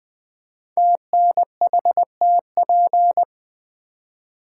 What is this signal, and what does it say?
Morse code, heard as a keyed tone, about 20 wpm, 700 Hz. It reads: TNHTP
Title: Code Groups - 5 Letters - 20wpm